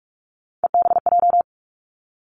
Morse code, heard as a keyed tone, about 46 wpm, 700 Hz. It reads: EBJ
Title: Three Challenging Letter-Number Combinations 45wpm